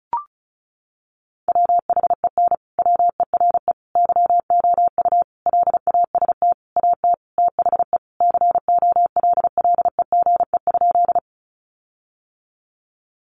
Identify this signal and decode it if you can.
Morse code, heard as a keyed tone, about 34 wpm, 700 Hz. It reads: WHEN WERE YOU LAST AT THE COLLEGE?
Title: Sentences from Top 1300 Words 35wpm